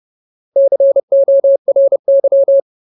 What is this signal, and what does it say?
Morse code, heard as a keyed tone, about 30 wpm, 550 Hz. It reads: CORY
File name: Ultimate Word List / 30WPM Ultimate